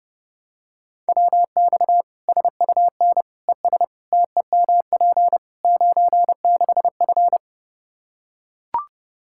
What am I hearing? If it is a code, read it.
Morse code, heard as a keyed tone, about 30 wpm, 700 Hz. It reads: WX SUN ES TEMP 96F
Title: Practice QSO - Exchange 2 - Info - 30wpm